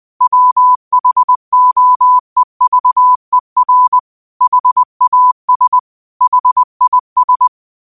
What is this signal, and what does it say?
Morse code, heard as a keyed tone, about 20 wpm, 1000 Hz. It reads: WHOEVER HAS HIS
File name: qotdfiles / 20wpm